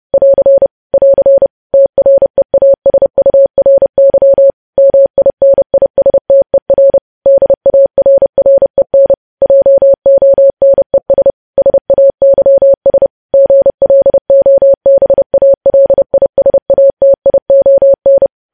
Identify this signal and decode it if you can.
Morse code, heard as a keyed tone, about 30 wpm, 550 Hz. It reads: <AR> <AR> TREASURY MINISTER DARREN JONES SAYS GLOBALISATION
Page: news headlines in morse code at 30 wpm